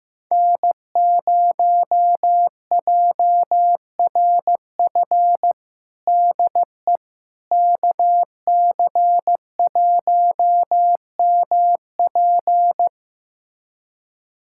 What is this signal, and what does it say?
Morse code, heard as a keyed tone, about 15 wpm, 700 Hz. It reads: N0JRF DE KC1MP
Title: USA Callsign DE Callsign - 5 Characters - 15wpm